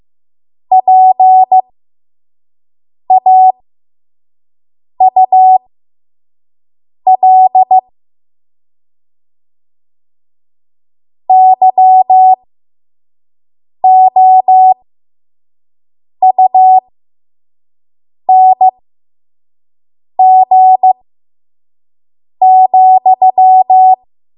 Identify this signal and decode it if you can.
Morse code, heard as a keyed tone, about 15 wpm, spaced as for 5 wpm, 750 Hz. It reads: PAUL YOUNG,
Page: 5 WPM Code Archive